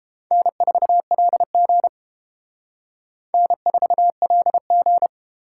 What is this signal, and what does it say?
Morse code, heard as a keyed tone, about 29 wpm, 700 Hz. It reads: N4LG N4LG